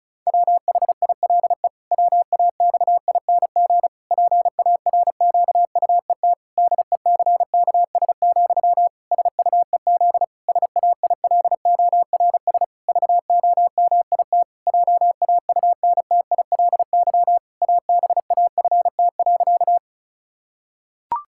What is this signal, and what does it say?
Morse code, heard as a keyed tone, about 34 wpm, 700 Hz. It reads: WHILE WAXING PARQUET DECKS, SUEZ SAILORS VOMIT JAUNTILY ABAFT.
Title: Pangram Challenge v2 - 35wpm